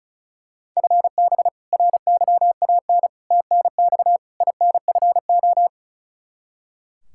Morse code, heard as a keyed tone, about 36 wpm, 700 Hz. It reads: FB RYAN TNX INFO
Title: Practice QSO - Exchange 3 - More Info - 35wpm